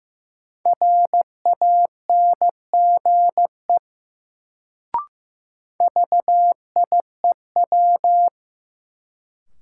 Morse code, heard as a keyed tone, about 15 wpm, 700 Hz. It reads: RANGE VIEW
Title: Sentences from Top 900 Words - New Word Drill - 15wpm